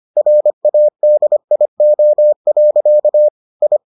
Morse code, heard as a keyed tone, about 25 wpm, 600 Hz. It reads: RADIO. I